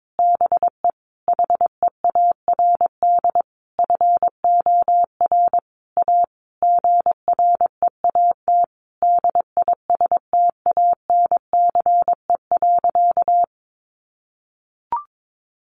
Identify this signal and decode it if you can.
Morse code, heard as a keyed tone, about 22 wpm, 700 Hz. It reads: BE HEARD FOR A GREAT DISTANCE.
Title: Sentences from Top 1200 Words 22wpm